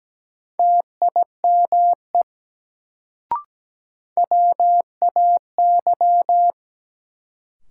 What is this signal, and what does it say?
Morse code, heard as a keyed tone, about 17 wpm, 700 Hz. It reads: TIME WAY
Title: Sentences from Top 100 Words - New Word Drill - 17wpm